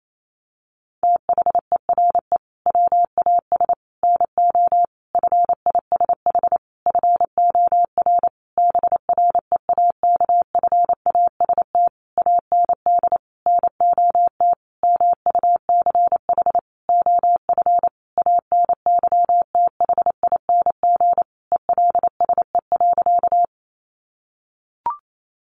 Morse code, heard as a keyed tone, about 28 wpm, 700 Hz. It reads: THERE WAS NO FISH FOR BREAKFAST AND NOT MUCH OF ANYTHING ELSE.
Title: Sentences from Top 1500 Words 28wpm